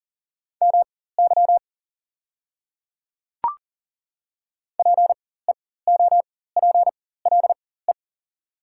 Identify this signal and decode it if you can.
Morse code, heard as a keyed tone, about 40 wpm, spaced as for 19 wpm, 700 Hz. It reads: MY PEOPLE
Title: 4x Character Spacing - Top 100 Words - 40wpm